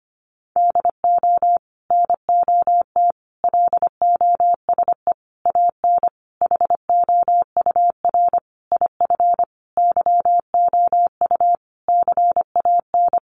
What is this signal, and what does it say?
Morse code, heard as a keyed tone, about 25 wpm, 700 Hz. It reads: DO NOT LOSE AN HOUR IF YOU CAN
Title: Sentences from Top 1100 Words 25wpm